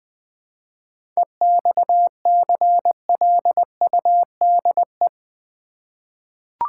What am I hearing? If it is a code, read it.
Morse code, heard as a keyed tone, about 20 wpm, 700 Hz. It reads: EXCLUDE
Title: 7 Letter Words - First 3 Letters Given - 20wpm